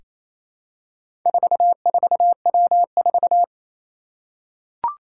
Morse code, heard as a keyed tone, about 27 wpm, 700 Hz. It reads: 44W4
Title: Four Challenging Letter-Number Combinations 28wpm